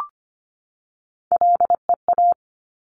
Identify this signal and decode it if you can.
Morse code, heard as a keyed tone, about 25 wpm, 700 Hz. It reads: LEA
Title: Top Three Letter Combinations in Words 25wpm